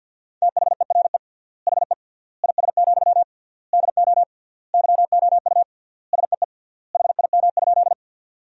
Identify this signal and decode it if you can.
Morse code, heard as a keyed tone, about 50 wpm, 700 Hz. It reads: THERE HE IS, DO YOU SEE HIM?
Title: 3x Word-Spacing - Sentences from Top 100 Words - 50wpm